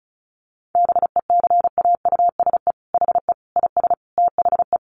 Morse code, heard as a keyed tone, about 34 wpm, 700 Hz. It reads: BECAUSE HE IS THE
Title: Sentences from Top 1500 Words 35wpm